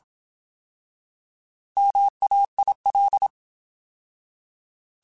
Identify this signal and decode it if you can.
Morse code, heard as a keyed tone, about 27 wpm, 800 Hz. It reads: MAIL
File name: Alphabetical / Alphabetical mp3s / Ms @25